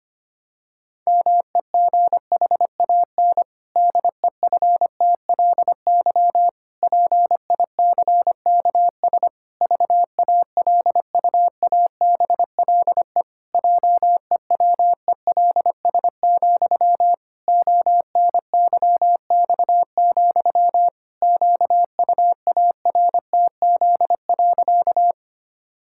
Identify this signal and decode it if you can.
Morse code, heard as a keyed tone, about 25 wpm, 700 Hz. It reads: MEGHAN DEFTLY PICKS VALUABLE JEWELS, ONYX, QUARTZ.
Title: Pangram Challenge v2 - 25wpm